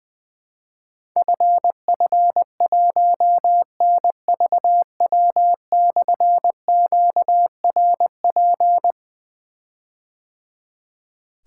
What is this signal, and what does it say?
Morse code, heard as a keyed tone, about 20 wpm, 700 Hz. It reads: FF1NVW/QRP
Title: Difficult Call Signs 20wpm